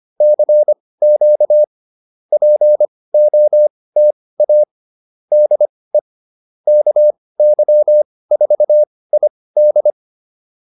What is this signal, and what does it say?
Morse code, heard as a keyed tone, about 25 wpm, spaced as for 18 wpm, 600 Hz. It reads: CQ POTA DE KY4ID